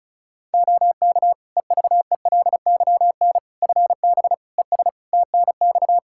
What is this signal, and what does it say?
Morse code, heard as a keyed tone, about 36 wpm, 700 Hz. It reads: OK EVELYN FB ES TNX
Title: Practice QSO - Exchange 2 - Info - 35wpm